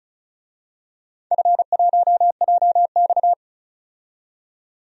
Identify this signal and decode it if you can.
Morse code, heard as a keyed tone, about 36 wpm, 700 Hz. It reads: F1JX
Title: DXCC-100 Call Signs - 4 Characters - 35wpm